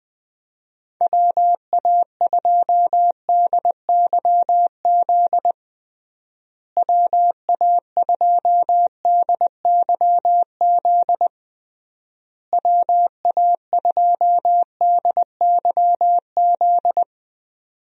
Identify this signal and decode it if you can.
Morse code, heard as a keyed tone, about 20 wpm, 700 Hz. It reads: WA2DYZ WA2DYZ WA2DYZ
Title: USA Call Signs - 6 Characters - Repeated 3x - 20wpm